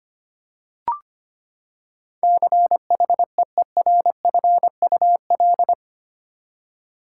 Morse code, heard as a keyed tone, about 25 wpm, 700 Hz. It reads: CHEERFUL
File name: Eight Letter Words - 25wpm